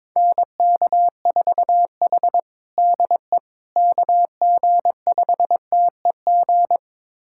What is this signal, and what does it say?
Morse code, heard as a keyed tone, about 21 wpm, 700 Hz. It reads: NK4H DE KG5TEG
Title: USA Callsign DE Callsign 22wpm